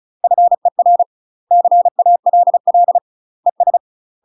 Morse code, heard as a keyed tone, about 35 wpm, 700 Hz. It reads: FER CALL ES
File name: UK MASTER QSOs TRAINING / 35WPM / UK MASTER QSOs 0011